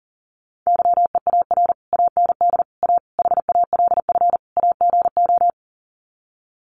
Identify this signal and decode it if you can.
Morse code, heard as a keyed tone, about 40 wpm, 700 Hz. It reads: YEAR AND A HALF AGO
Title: Sets of 5 Words Encore 40wpm